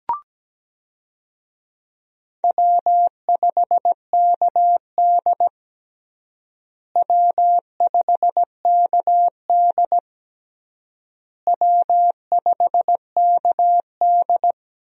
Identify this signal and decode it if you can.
Morse code, heard as a keyed tone, about 17 wpm, 700 Hz. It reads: W5KD W5KD W5KD
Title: USA Call Signs - 4 Characters - Repeated 3x - 17wpm